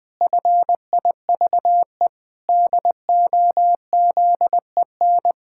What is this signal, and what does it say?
Morse code, heard as a keyed tone, about 20 wpm, 700 Hz. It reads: FIVE DOZEN
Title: Pangram Challenge v2 - 20wpm